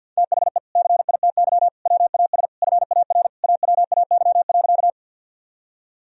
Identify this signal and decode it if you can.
Morse code, heard as a keyed tone, about 50 wpm, 700 Hz. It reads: THE CITY WAS FAR AWAY.
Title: Sentences from Top 1000 Words - Review Part 1 - 50wpm